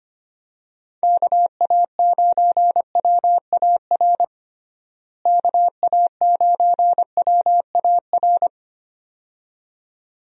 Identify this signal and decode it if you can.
Morse code, heard as a keyed tone, about 25 wpm, 700 Hz. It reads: KA9WAR KA9WAR
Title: USA Call Signs - 6 Characters - Repeated 2x - 25wpm